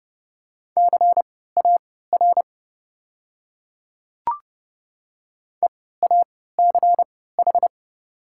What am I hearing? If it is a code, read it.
Morse code, heard as a keyed tone, about 30 wpm, spaced as for 17 wpm, 700 Hz. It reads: CAR EACH
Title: 3x Character Spacing - Top 300 Words - 30wpm